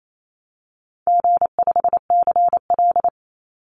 Morse code, heard as a keyed tone, about 28 wpm, 700 Hz. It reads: G5CL
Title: Easy Call Signs - 28wpm